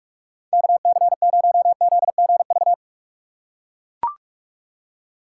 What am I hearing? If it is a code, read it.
Morse code, heard as a keyed tone, about 44 wpm, 700 Hz. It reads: KC0ZGV